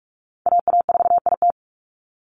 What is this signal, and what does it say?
Morse code, heard as a keyed tone, about 46 wpm, 700 Hz. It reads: AA4IT
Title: NA SOTA Chaser Callsigns 45wpm